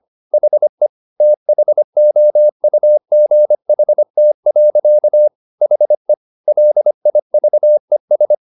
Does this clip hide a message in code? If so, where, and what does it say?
Yes: Morse code from 0.3 to 8.3 s, heard as a keyed tone, about 25 wpm, 600 Hz. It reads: HE THOUGHT. HE LIVES